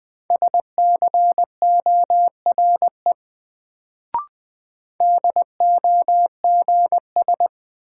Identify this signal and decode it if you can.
Morse code, heard as a keyed tone, about 20 wpm, 700 Hz. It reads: SCORE DOGS
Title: Sentences from Top 1600 Words - New Word Drill Alt - 20wpm